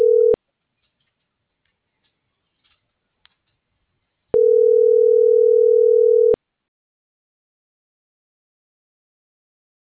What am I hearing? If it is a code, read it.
Ringback tone (North America), 440 + 480 Hz.
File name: CENTRAL MULTIMODAL / out kimberly.suarez